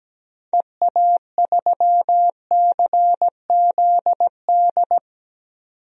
Morse code, heard as a keyed tone, about 17 wpm, 700 Hz. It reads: EA3CZD
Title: DXCC-100 Call Signs - 6 Characters - 17wpm